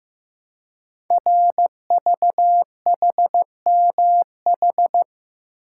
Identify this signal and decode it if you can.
Morse code, heard as a keyed tone, about 15 wpm, 700 Hz. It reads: RVHMH
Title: Code Groups - 5 Letters - 15wpm